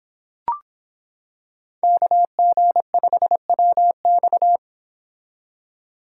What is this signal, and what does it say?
Morse code, heard as a keyed tone, about 26 wpm, 700 Hz. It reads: KG5WX